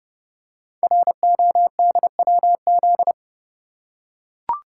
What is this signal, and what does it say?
Morse code, heard as a keyed tone, about 30 wpm, 700 Hz. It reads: RODWZ
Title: Code Groups - 5 Letters - 30wpm